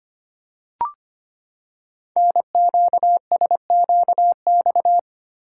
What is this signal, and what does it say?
Morse code, heard as a keyed tone, about 25 wpm, 700 Hz. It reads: NQSQX
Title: Code Groups - 5 Letters - 25wpm